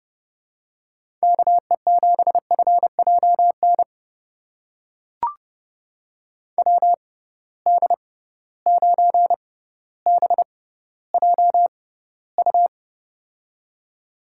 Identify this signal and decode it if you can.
Morse code, heard as a keyed tone, about 32 wpm, spaced as for 16 wpm, 700 Hz. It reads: KE7FJN WD9BJU
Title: USA Call Signs - 6 Characters - 6x Character Spacing - 30wpm